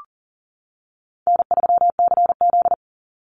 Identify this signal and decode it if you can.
Morse code, heard as a keyed tone, about 40 wpm, 700 Hz. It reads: N3CZ
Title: Easy Call Signs - 40wpm